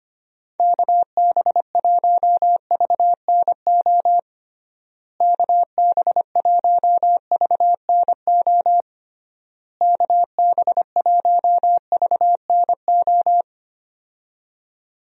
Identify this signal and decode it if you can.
Morse code, heard as a keyed tone, about 25 wpm, 700 Hz. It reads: KB1VNO KB1VNO KB1VNO
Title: USA Call Signs - 6 Characters - Repeated 3x - 25wpm